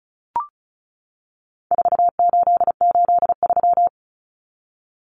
Morse code, heard as a keyed tone, about 34 wpm, 700 Hz. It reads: 4883
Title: Numbers - 4-Digit 35wpm